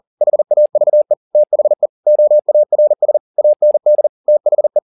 Morse code, heard as a keyed tone, about 40 wpm, 600 Hz. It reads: HAVE THE OARS AND THE
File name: the old man and the sea 2 40wpm